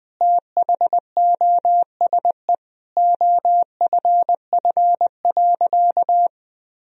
Morse code, heard as a keyed tone, about 20 wpm, 700 Hz. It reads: THOSE OFF.
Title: Sentences from Top 1000 Words - Review Part 1 - 20wpm